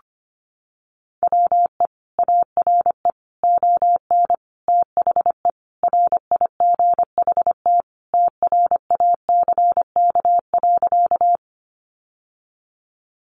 Morse code, heard as a keyed tone, about 25 wpm, 700 Hz. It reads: WE ARE ON THE RIGHT TRACK.
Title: Sentences from Top 1100 Words 25wpm